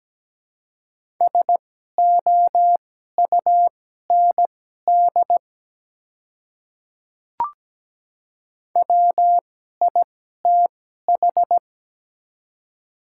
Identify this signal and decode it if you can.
Morse code, heard as a keyed tone, about 17 wpm, spaced as for 12 wpm, 700 Hz. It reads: SOUND WITH
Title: 2x Character Spacing - Top 100 Words - 17wpm